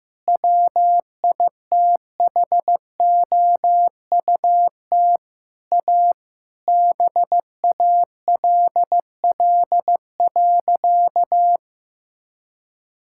Morse code, heard as a keyed tone, about 15 wpm, 700 Hz. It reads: WITHOUT A BALL.
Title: Sentences from Top 1000 Words - Review Part 6 - 15wpm